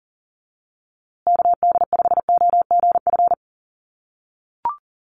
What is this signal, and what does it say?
Morse code, heard as a keyed tone, about 40 wpm, 700 Hz. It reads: KD5OGF
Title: USA Call Signs Repeated 2x 40wpm